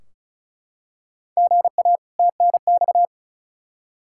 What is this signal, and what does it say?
Morse code, heard as a keyed tone, about 36 wpm, 700 Hz. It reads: GA TNX
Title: Practice QSO - Exchange 1 - Beginning - 35wpm